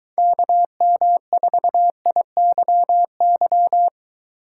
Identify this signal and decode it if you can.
Morse code, heard as a keyed tone, about 23 wpm, 700 Hz. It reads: KM4IYY